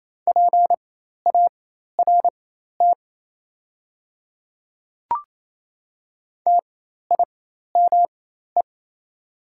Morse code, heard as a keyed tone, about 28 wpm, spaced as for 13 wpm, 690 Hz. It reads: PART TIME
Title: 4x Character Spacing - Top 200 Words - 28wpm